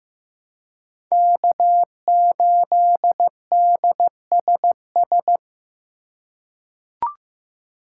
Morse code, heard as a keyed tone, about 15 wpm, 700 Hz. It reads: K8DSS
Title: K1USN SST QSO Practice - 15wpm